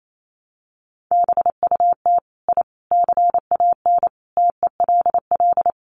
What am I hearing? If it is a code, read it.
Morse code, heard as a keyed tone, about 28 wpm, 700 Hz. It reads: BUT I CAN TELL
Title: Sets of 5 Words Encore 28wpm